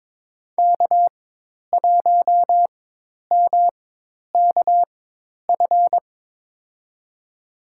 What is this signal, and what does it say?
Morse code, heard as a keyed tone, about 22 wpm, spaced as for 10 wpm, 700 Hz. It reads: K1MKF
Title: USA Call Signs - 5 Characters - 4x Character Spacing - 22wpm